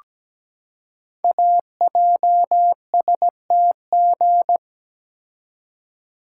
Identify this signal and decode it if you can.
Morse code, heard as a keyed tone, about 17 wpm, 700 Hz. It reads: AJSTG
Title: Code Groups - 5 Letters - 17wpm